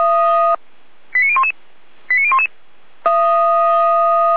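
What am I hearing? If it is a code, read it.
Five-tone selective call (ZVEI), heard as quick runs of five tones, tones of about 80 ms. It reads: 89011 89011
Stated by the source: ZVEI five-tone call received on SDR.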